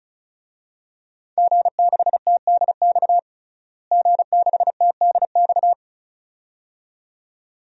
Morse code, heard as a keyed tone, about 34 wpm, 700 Hz. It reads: G6TDX G6TDX
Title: DXCC-100 Call Signs - 5 Characters - Repeated 2x - 35wpm